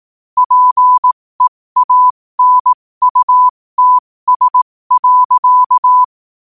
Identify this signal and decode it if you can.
Morse code, heard as a keyed tone, about 18 wpm, spaced as for 15 wpm, 1000 Hz. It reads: PEANUTS.
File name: qotdfiles / 15wpm